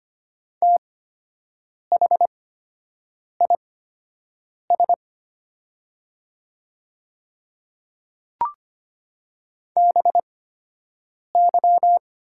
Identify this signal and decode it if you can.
Morse code, heard as a keyed tone, about 25 wpm, spaced as for 7 wpm, 700 Hz. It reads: THIS BY
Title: 8x Character Spacing - Top 100 Words - 25wpm